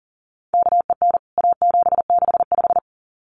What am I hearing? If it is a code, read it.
Morse code, heard as a keyed tone, about 40 wpm, 700 Hz. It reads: KEN A765
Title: CWT Exchange 40wpm